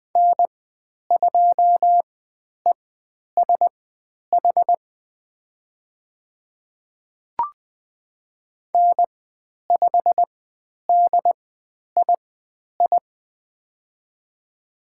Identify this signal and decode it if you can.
Morse code, heard as a keyed tone, about 20 wpm, spaced as for 10 wpm, 700 Hz. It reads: N2ESH N5DII
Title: 102 Lesson 17g - Letter u - Call Signs 20wpm